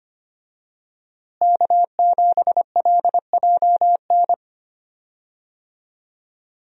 Morse code, heard as a keyed tone, about 25 wpm, 700 Hz. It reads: K7LJN
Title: USA Call Signs - 5 Characters - Repeated 2x - 25wpm